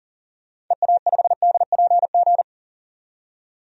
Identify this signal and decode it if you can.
Morse code, heard as a keyed tone, about 40 wpm, 700 Hz. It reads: EA5DPG
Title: DXCC-100 Call Signs - 6 Characters - 40wpm